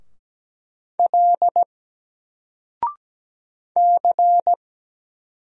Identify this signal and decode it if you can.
Morse code, heard as a keyed tone, about 17 wpm, 700 Hz. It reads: L C
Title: Single Letter-Number 17wpm